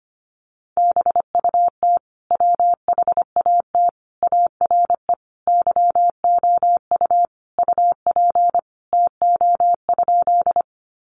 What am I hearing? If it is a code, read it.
Morse code, heard as a keyed tone, about 25 wpm, 700 Hz. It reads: BUT WHAT ARE YOU UP TO?
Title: Sentences from Top 100 Words 25wpm